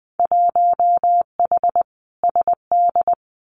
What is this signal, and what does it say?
Morse code, heard as a keyed tone, about 20 wpm, 700 Hz. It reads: 1H SD
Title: Winter Field Day Exchanges 20wpm